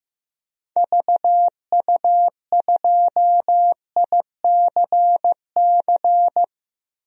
Morse code, heard as a keyed tone, about 15 wpm, 700 Hz. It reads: VU2ICC
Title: DXCC-100 Call Signs - 6 Characters - 15wpm